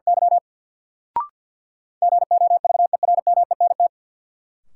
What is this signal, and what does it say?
Morse code, heard as a keyed tone, about 51 wpm, 700 Hz. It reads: Y GOVERMENT
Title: Sentences from Top 1300 Words - New Word Drill - 50wpm